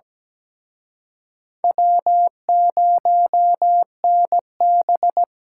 Morse code, heard as a keyed tone, about 20 wpm, 700 Hz. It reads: W0NB
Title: Speed Racing - USA 1x2 Call Signs 17wpm